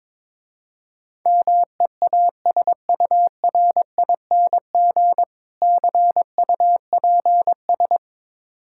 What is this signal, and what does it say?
Morse code, heard as a keyed tone, about 22 wpm, 700 Hz. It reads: MEASURING CUPS
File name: Easy Words - Speed-Racing with Hints - 22wpm